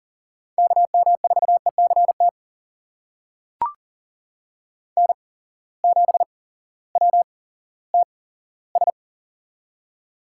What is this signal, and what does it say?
Morse code, heard as a keyed tone, about 40 wpm, spaced as for 18 wpm, 700 Hz. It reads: KM4ECT N7WTS
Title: USA Call Signs - 8x Character Spacing - 40wpm